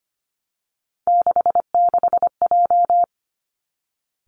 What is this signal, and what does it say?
Morse code, heard as a keyed tone, about 25 wpm, 700 Hz. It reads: 66J
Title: Three Challenging Letter-Number Combinations 25wpm